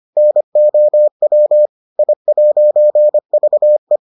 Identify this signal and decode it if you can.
Morse code, heard as a keyed tone, about 25 wpm, 600 Hz. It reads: NOW I'VE